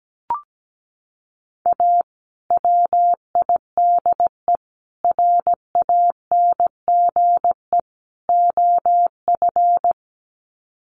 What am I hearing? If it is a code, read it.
Morse code, heard as a keyed tone, about 17 wpm, 700 Hz. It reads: A WIDE RANGE OF
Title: Sets of 4 Words 17wpm